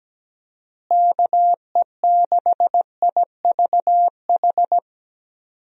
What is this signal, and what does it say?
Morse code, heard as a keyed tone, about 17 wpm, 700 Hz. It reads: KE6IVH
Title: USA Call Signs - 6 Characters - Repeated 2x - 17wpm